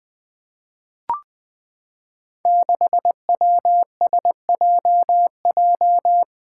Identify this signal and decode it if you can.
Morse code, heard as a keyed tone, about 20 wpm, 700 Hz. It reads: 6WSJJ
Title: Five Challenging Letter-Number Combinations 20wpm